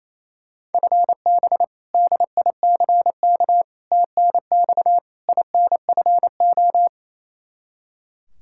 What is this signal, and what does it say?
Morse code, heard as a keyed tone, about 28 wpm, 700 Hz. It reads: FB DICK TNX INFO
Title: Practice QSO - Exchange 3 - More Info - 28wpm